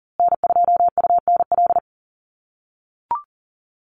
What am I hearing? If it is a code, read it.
Morse code, heard as a keyed tone, about 40 wpm, 700 Hz. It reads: N2UNL
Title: USA Call Signs Repeated 2x 40wpm